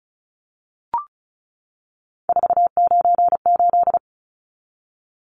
Morse code, heard as a keyed tone, about 34 wpm, 700 Hz. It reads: 498
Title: Numbers - 3-Digit 35wpm